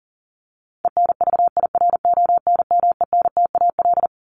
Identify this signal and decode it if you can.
Morse code, heard as a keyed tone, about 40 wpm, 700 Hz. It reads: ENVIRONMENTAL
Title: Top 1001-1250 Words 40wpm